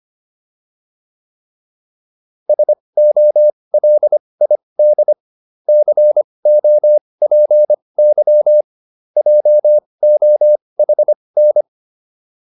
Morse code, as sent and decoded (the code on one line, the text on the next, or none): ... --- .-.. .. -.. / -.-. --- .--. -.-- / .--- --- .... -.
SOLID COPY JOHN